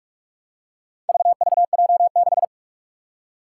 Morse code, heard as a keyed tone, about 42 wpm, 700 Hz. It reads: VVJ6
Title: Four Challenging Letter-Number Combinations 45wpm